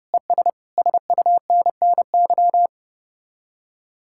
Morse code, heard as a keyed tone, about 33 wpm, 700 Hz. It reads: ES SUNNY